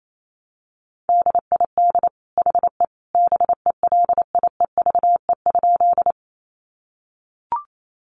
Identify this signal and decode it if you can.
Morse code, heard as a keyed tone, about 28 wpm, 700 Hz. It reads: DID HE BELIEVE?
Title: Sentences from Top 1000 Words 28wpm